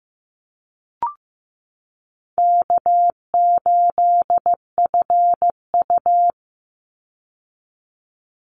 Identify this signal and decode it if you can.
Morse code, heard as a keyed tone, about 15 wpm, 700 Hz. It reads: K8FU
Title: Easy Call Signs Repeated 3x - 15wpm